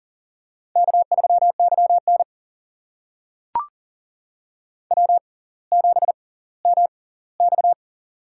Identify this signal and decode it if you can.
Morse code, heard as a keyed tone, about 40 wpm, spaced as for 21 wpm, 690 Hz. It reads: K3YN W7MX
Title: USA Call Signs - 4 Characters - 6x Character Spacing - 40wpm